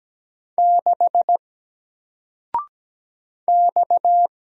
Morse code, heard as a keyed tone, about 17 wpm, 700 Hz. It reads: 6 X
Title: Single Character (Letter:Number:Punctuation:Prosign) Alt 17wpm